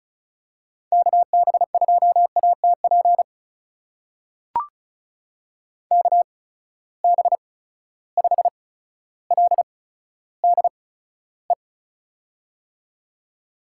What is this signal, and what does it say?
Morse code, heard as a keyed tone, about 36 wpm, spaced as for 15 wpm, 700 Hz. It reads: KB2ATP KB5LDE
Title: USA Call Signs - 8x Character Spacing - 35wpm